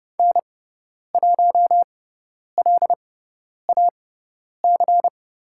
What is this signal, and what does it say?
Morse code, heard as a keyed tone, about 30 wpm, spaced as for 10 wpm, 690 Hz. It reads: N1LAC
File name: Lesson 17g - Letter u - Call Signs 30wpm